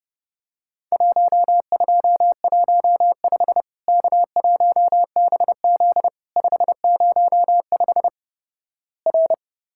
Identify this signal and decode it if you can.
Morse code, heard as a keyed tone, about 30 wpm, 700 Hz. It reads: 1215 K1BZ 505 R
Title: Worked All Europe DX Contest - QTC Exchange - 30wpm